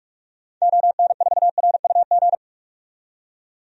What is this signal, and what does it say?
Morse code, heard as a keyed tone, about 46 wpm, 700 Hz. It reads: ON4RUG
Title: DXCC-100 Call Signs - 6 Characters - 45wpm